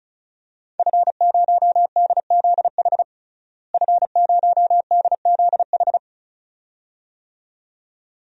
Morse code, heard as a keyed tone, about 34 wpm, 700 Hz. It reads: F0DZH F0DZH
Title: DXCC-100 Call Signs - 5 Characters - Repeated 2x - 35wpm